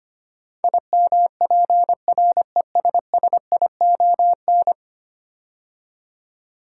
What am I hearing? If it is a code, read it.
Morse code, heard as a keyed tone, about 25 wpm, 700 Hz. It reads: IMPRESSION
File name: 10 Letter Words - First 3 Letters Given - 25wpm